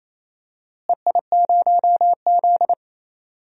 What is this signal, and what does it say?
Morse code, heard as a keyed tone, about 28 wpm, 700 Hz. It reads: EI0Z
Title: DXCC-100 Call Signs - 4 Characters - Repeated 2x - 28wpm